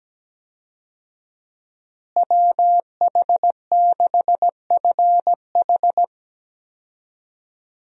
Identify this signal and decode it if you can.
Morse code, heard as a keyed tone, about 17 wpm, 700 Hz. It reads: WH6FH
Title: USA Call Signs - 5 Characters - Repeated 3x - 17wpm